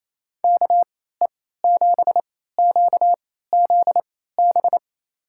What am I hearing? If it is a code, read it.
Morse code, heard as a keyed tone, about 28 wpm, spaced as for 16 wpm, 700 Hz. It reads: KE7QZB
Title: USA Call Signs - 6 Characters - 3x Character Spacing - 28wpm